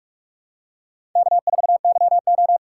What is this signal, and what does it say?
Morse code, heard as a keyed tone, about 47 wpm, 700 Hz. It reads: K4YO
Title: USA Callsign DE Callsign - 4 Characters - 45wpm